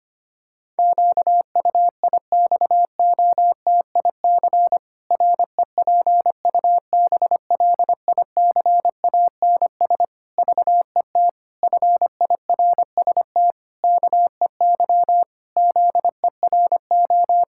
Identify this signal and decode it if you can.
Morse code, heard as a keyed tone, about 25 wpm, 700 Hz. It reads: QUIXOTIC REPUBLICANS VET FIRST KEY ZERO
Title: Pangram Challenge v2 - 25wpm